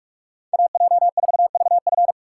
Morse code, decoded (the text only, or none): AJ4VF